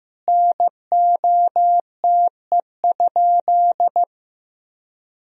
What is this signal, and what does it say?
Morse code, heard as a keyed tone, about 15 wpm, 700 Hz. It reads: NOTE?
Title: Sentences from Top 1000 Words - Review Part 2 - 15wpm